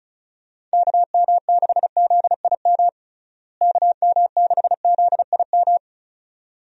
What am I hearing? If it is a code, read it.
Morse code, heard as a keyed tone, about 36 wpm, 700 Hz. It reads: KM6ZIM KM6ZIM
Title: USA Call Signs - 6 Characters - Repeated 2x - 35wpm